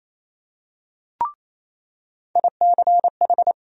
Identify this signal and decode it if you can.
Morse code, heard as a keyed tone, about 28 wpm, 700 Hz. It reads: ICH
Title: USA Call Sign Postfixes - 3 Letters 28wpm